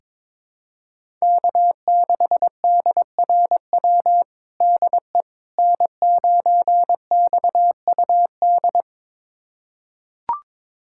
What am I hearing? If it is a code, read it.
Morse code, heard as a keyed tone, about 22 wpm, 700 Hz. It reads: K6DRW DE N9XUD
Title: USA Callsign DE Callsign - 5 Characters - 22wpm